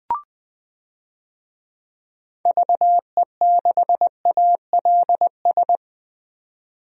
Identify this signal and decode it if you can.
Morse code, heard as a keyed tone, about 20 wpm, 700 Hz. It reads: VE6ALS
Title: DXCC-100 Call Signs - 6 Characters - 20wpm